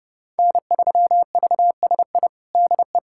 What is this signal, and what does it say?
Morse code, heard as a keyed tone, about 30 wpm, 700 Hz. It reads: N3VSI DE
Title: USA Callsign DE Callsign - 5 Characters - 30wpm